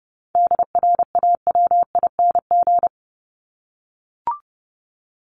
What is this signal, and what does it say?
Morse code, heard as a keyed tone, about 30 wpm, 700 Hz. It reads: DRAWING
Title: ING Suffix - Top 100 Words with Base Words 30wpm